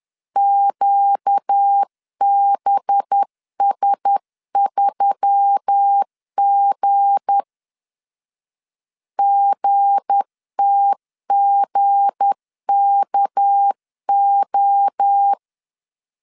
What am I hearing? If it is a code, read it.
Morse code, heard as a keyed tone, about 10 wpm, 800 Hz. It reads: QBS3G GTGKO
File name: LEZIONI / 7 GF9Z / 7 G 10wpm 2501